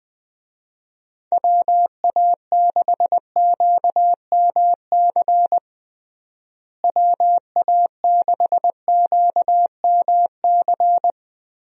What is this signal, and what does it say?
Morse code, heard as a keyed tone, about 20 wpm, 700 Hz. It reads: WA6QMC WA6QMC
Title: USA Call Signs - 6 Characters - Repeated 2x - 20wpm